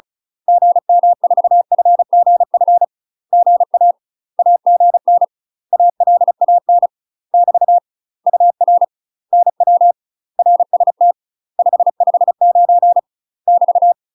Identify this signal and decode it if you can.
Morse code, heard as a keyed tone, about 34 wpm, 700 Hz. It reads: GM4FGF GA AGN ALAN <BT> UR NW RST 559 <BT>
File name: UK MASTER QSOs TRAINING / 35WPM / UK MASTER QSOs 0043